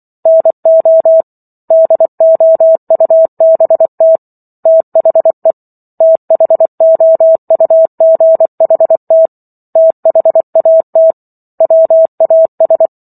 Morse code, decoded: NO DOUBT THE THOUGHT THAT WAS

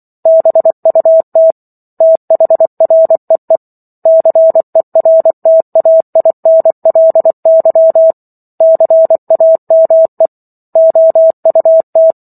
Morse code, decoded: BUT THREE CERTAINLY CAME OUT

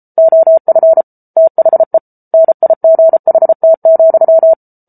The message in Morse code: OF THE NIGHT,

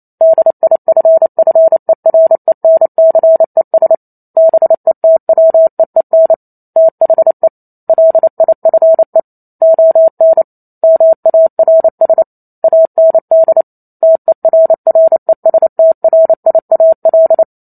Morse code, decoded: DIFFERENCES BETWEEN THE LIFE ON MARS AND TERRESTRIAL

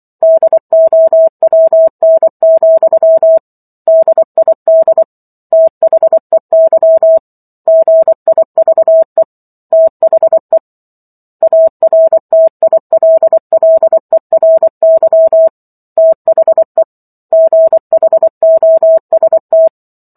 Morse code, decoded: DOWN, DID THEY GIVE THE ARTILLERY THE GHOST

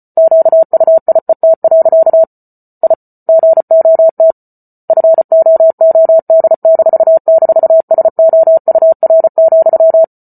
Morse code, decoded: QUIET. I GOT FOOD--SOUR,

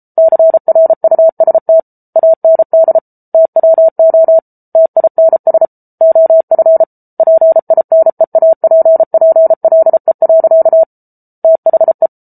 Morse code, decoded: CRUST AND TWO TINS OF PINEAPPLE. THE

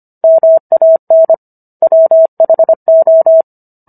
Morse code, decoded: MAN WHO

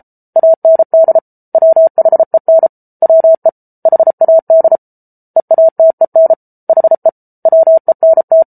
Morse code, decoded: AND WHEN WE HAD EATEN HE WENT

